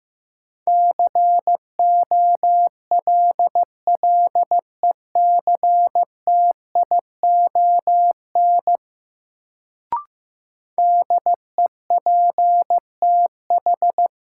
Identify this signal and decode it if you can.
Morse code, heard as a keyed tone, about 15 wpm, 700 Hz. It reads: COLLECTION DEPTH
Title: Sentences from Top 1200 Words - New Word Drill Alt - 15wpm